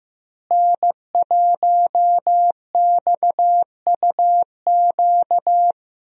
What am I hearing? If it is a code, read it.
Morse code, heard as a keyed tone, about 15 wpm, 700 Hz. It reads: N1XUQ
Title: USA Call Signs - 5 Characters - Repeated 3x - 15wpm